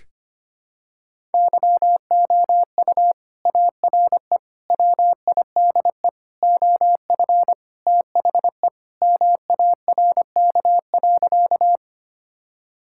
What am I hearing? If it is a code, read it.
Morse code, heard as a keyed tone, about 25 wpm, 700 Hz. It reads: YOU ARE WIDE OF THE MARK.
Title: Sentences from Top 1000 Words - Review Part 6 - 25wpm